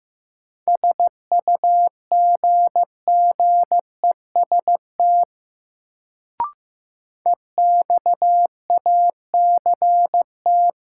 Morse code, heard as a keyed tone, about 15 wpm, 700 Hz. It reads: SUGGEST EXACT
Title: Sentences from Top 1300 Words - New Word Drill Alt - 15wpm